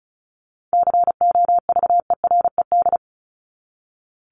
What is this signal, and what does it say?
Morse code, heard as a keyed tone, about 34 wpm, 700 Hz. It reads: COVERED
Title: ED Suffix - Top 100 Words with Base Words 35wpm